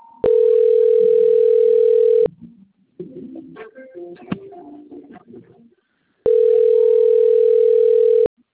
Ringback tone (North America), 440 + 480 Hz.